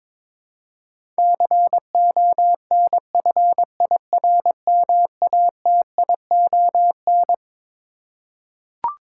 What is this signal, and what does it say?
Morse code, heard as a keyed tone, about 22 wpm, 700 Hz. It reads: CONFIRMATION
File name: Twelve Letter Words - 22wpm